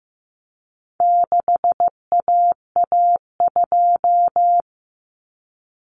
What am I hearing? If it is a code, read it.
Morse code, heard as a keyed tone, about 15 wpm, 700 Hz. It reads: 6AA2
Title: Common Cut Numbers - 4-Digit 15wpm